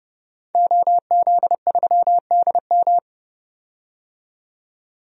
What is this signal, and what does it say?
Morse code, heard as a keyed tone, about 30 wpm, 700 Hz. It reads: OZ3DM
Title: DXCC-100 Call Signs - 5 Characters - 30wpm